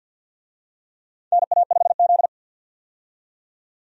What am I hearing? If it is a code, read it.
Morse code, heard as a keyed tone, about 47 wpm, 700 Hz. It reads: NA5Z
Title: USA Call Signs - 4 Characters - 50wpm